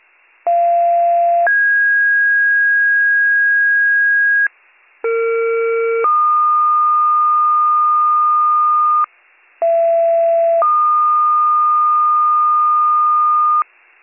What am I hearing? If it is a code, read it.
Two-tone page, heard as a one-second tone, then a three-second tone, about 690 then 1650 Hz, about 460 then 1220 Hz, about 670 then 1220 Hz. It is heard from 0.5 to 13.6 s.